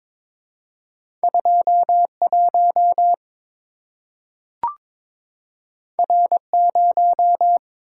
Morse code, heard as a keyed tone, about 22 wpm, 700 Hz. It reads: 21 R0
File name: practice / 22wpm / Two Letter-Number Combinations - 22wpm